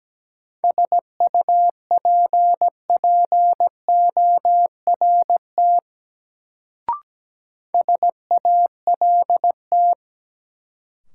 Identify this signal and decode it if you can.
Morse code, heard as a keyed tone, about 17 wpm, 700 Hz. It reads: SUPPORT SALT
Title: Sentences from Top 800 Words - New Word Drill - 17wpm